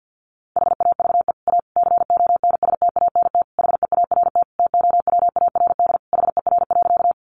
Morse code, heard as a keyed tone, about 50 wpm, 700 Hz. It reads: HAVE A CONSTANT HEART TOWARD HER.
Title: Sentences from Top 1500 Words 50wpm